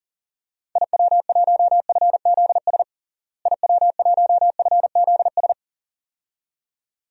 Morse code, heard as a keyed tone, about 37 wpm, 700 Hz. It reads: IW1FZS IW1FZS
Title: DXCC-100 Call Signs - 6 Characters - Repeated 2x - 40wpm